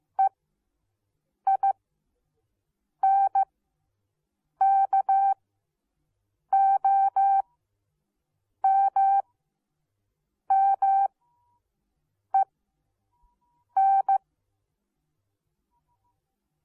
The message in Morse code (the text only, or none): EINKOMMEN